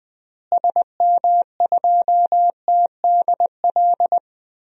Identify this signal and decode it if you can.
Morse code, heard as a keyed tone, about 20 wpm, 700 Hz. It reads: SM2TDL
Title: DXCC-100 Call Signs - 6 Characters - Repeated 3x - 20wpm